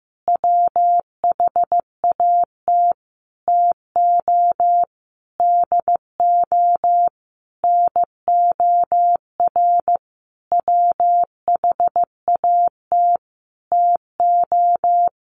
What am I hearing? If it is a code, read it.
Morse code, heard as a keyed tone, about 15 wpm, 700 Hz. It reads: WHAT TO DO NOR WHAT TO
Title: Sentences from Top 1500 Words 15wpm